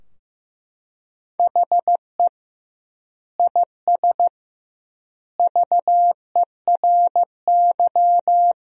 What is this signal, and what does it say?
Morse code, heard as a keyed tone, about 15 wpm, 700 Hz. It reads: HE IS VERY
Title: 2x Word-Spacing - Sentences from Top 400 Words - 15wpm